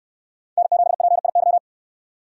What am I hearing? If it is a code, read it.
Morse code, heard as a keyed tone, about 69 wpm, 700 Hz. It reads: N7PIY